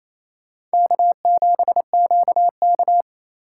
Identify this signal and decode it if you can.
Morse code, heard as a keyed tone, about 36 wpm, 700 Hz. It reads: K7QK